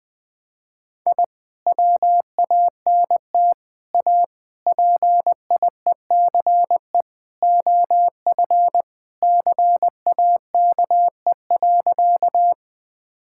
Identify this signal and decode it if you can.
Morse code, heard as a keyed tone, about 20 wpm, 700 Hz. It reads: I WANT A PIECE OF CAKE.
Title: Sentences with 3-Word Phrases - 20wpm